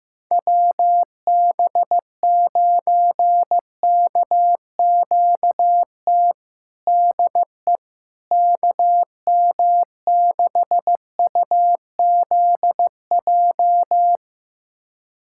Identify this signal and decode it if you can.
Morse code, heard as a keyed tone, about 15 wpm, 700 Hz. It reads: WB9KQT DE KM6UZJ
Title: USA Callsign DE Callsign - 6 Characters - 15wpm